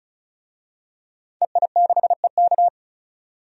Morse code, heard as a keyed tone, about 36 wpm, 700 Hz. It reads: EI6EK
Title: DXCC-100 Call Signs - 5 Characters - 35wpm